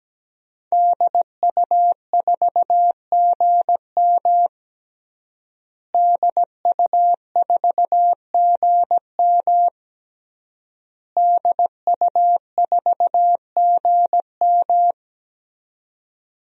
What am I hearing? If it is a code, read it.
Morse code, heard as a keyed tone, about 17 wpm, 700 Hz. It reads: DU4GM DU4GM DU4GM
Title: DXCC-100 Call Signs - 5 Characters - Repeated 3x - 17wpm